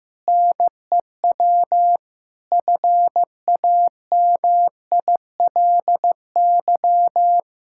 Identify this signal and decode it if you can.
Morse code, heard as a keyed tone, about 15 wpm, 700 Hz. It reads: NEW FAMILY
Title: Sentences from Top 1600 Words 15wpm